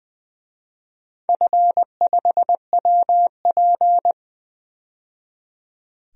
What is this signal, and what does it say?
Morse code, heard as a keyed tone, about 20 wpm, 700 Hz. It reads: F5WP
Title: DXCC-100 Call Signs - 4 Characters - 20wpm